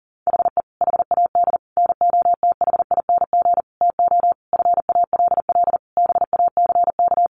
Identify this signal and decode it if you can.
Morse code, heard as a keyed tone, about 40 wpm, 700 Hz. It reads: HE HAD NOTHING TO FALL BACK
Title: Sentences from Top 1100 Words 40wpm